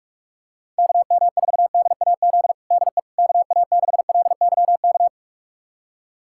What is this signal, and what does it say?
Morse code, heard as a keyed tone, about 42 wpm, 700 Hz. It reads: KM4DAZ DE KA6LYK
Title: USA Callsign DE Callsign 45wpm